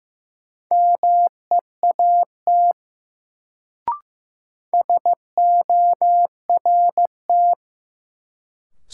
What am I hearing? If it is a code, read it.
Morse code, heard as a keyed tone, about 16 wpm, 700 Hz. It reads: MEAT SORT
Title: Sentences from Top 1000 Words - Review Part 3 - New Word Drill - 15wpm